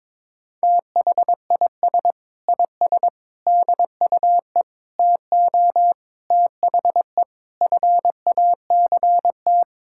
Morse code, heard as a keyed tone, about 21 wpm, 700 Hz. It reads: THIS IS DUE TO THE FACT
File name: practice / 22wpm / Sentences with 8-Word Phrases - 22wpm